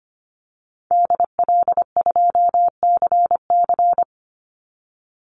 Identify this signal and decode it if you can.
Morse code, heard as a keyed tone, about 25 wpm, 700 Hz. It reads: DL2CC
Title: Call Signs Repeated 2x - 25wpm